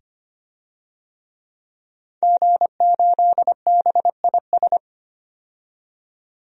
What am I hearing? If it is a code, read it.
Morse code, heard as a keyed tone, about 25 wpm, 700 Hz. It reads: G8BIS
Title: DXCC-100 Call Signs - 5 Characters - 25wpm